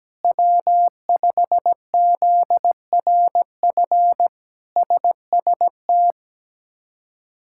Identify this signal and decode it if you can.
Morse code, heard as a keyed tone, about 17 wpm, 700 Hz. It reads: W5ZRF SST
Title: K1USN SST QSO Practice - 17wpm